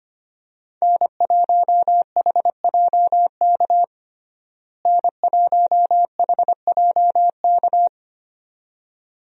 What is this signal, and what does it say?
Morse code, heard as a keyed tone, about 25 wpm, 690 Hz. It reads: N1HJK N1HJK
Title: USA Call Signs - 5 Characters - Repeated 2x - 25wpm